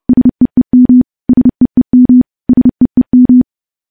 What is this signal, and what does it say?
Morse code, heard as a keyed tone, about 32 wpm, 250 Hz. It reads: SEEM SEEM SEEM
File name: morse / morse-training-30WPM